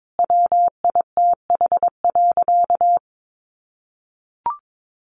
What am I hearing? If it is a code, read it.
Morse code, heard as a keyed tone, about 22 wpm, 700 Hz. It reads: WITH.
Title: Sentences from Top 1500 Words 22wpm